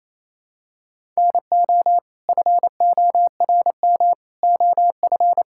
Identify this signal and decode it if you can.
Morse code, heard as a keyed tone, about 28 wpm, 700 Hz. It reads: NO FORM OF
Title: Sentences from Top 1000 Words - Review Part 6 - 28wpm